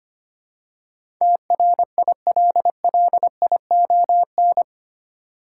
Morse code, heard as a keyed tone, about 25 wpm, 700 Hz. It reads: TRILLION
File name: Eight Letter Words - 25wpm